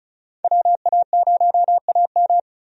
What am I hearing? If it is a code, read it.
Morse code, heard as a keyed tone, about 36 wpm, 700 Hz. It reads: WA0AM
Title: USA Call Signs - 5 Characters - 35wpm